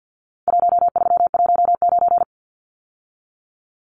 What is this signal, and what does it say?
Morse code, heard as a keyed tone, about 48 wpm, 700 Hz. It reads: 1319
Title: Numbers - Contest Style - 4-Digit 50wpm